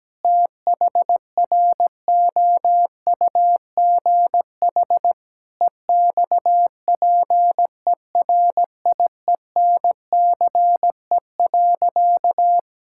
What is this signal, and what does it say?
Morse code, heard as a keyed tone, about 17 wpm, 700 Hz. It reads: THROUGH EXPERIENCE.
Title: Sentences from Top 1000 Words - Review Part 5 - 17wpm